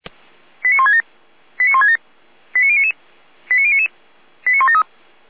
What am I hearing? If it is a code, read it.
Five-tone selective call (ZVEI), heard as quick runs of five tones, tones of about 80 ms. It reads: 89167 89167 89099 89099 89173